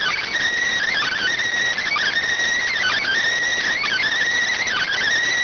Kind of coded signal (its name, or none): Emergency Alert System (SAME) tones